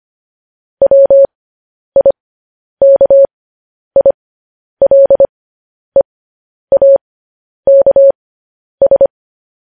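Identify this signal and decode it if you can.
Morse code, heard as a keyed tone, about 25 wpm, spaced as for 10 wpm, 550 Hz. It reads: WIKILEAKS
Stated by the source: Morse code practice sending news headlines at 10 wpm.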